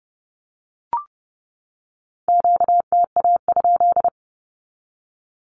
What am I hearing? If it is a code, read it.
Morse code, heard as a keyed tone, about 30 wpm, 700 Hz. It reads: QTA?
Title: Q-Codes 30wpm